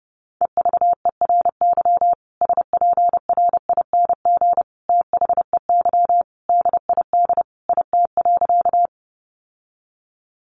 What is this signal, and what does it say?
Morse code, heard as a keyed tone, about 30 wpm, 700 Hz. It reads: EVERY SPRING THEY DID IT.
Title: Sentences from Top 1200 Words 30wpm